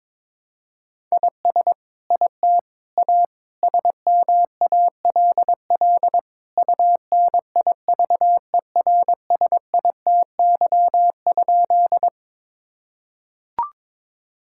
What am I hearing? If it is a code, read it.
Morse code, heard as a keyed tone, about 21 wpm, 700 Hz. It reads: IS IT A SMALL UNIVERSITY?
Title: Sentences from Top 1000 Words - Review Part 5 - 22wpm